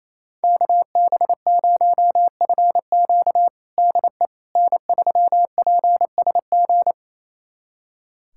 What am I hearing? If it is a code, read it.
Morse code, heard as a keyed tone, about 28 wpm, 700 Hz. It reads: KB0FQ DE N3PSG
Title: USA Callsign DE Callsign - 5 Characters - 28wpm